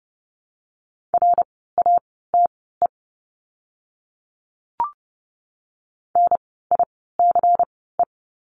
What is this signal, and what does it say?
Morse code, heard as a keyed tone, about 30 wpm, spaced as for 17 wpm, 700 Hz. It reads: RATE NICE
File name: Lesson 16f - Letter c - Words 30wpm